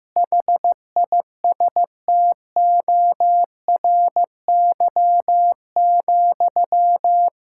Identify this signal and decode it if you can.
Morse code, heard as a keyed tone, about 15 wpm, 700 Hz. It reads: HISTORY,
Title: Sentences from Top 1000 Words - Review Part 5 - 15wpm